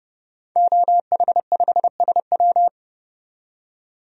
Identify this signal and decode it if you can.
Morse code, heard as a keyed tone, about 30 wpm, 700 Hz. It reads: OH5SW
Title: DXCC-100 Call Signs - 5 Characters - 30wpm